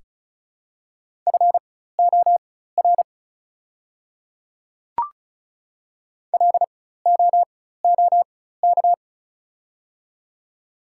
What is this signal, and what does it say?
Morse code, heard as a keyed tone, about 36 wpm, spaced as for 16 wpm, 700 Hz. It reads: FOR LOOK
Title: 4x Character Spacing - Top 100 Words - 35wpm